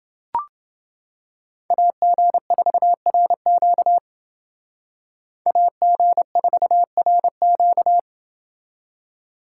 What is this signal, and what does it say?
Morse code, heard as a keyed tone, about 29 wpm, 700 Hz. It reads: AG4RQ AG4RQ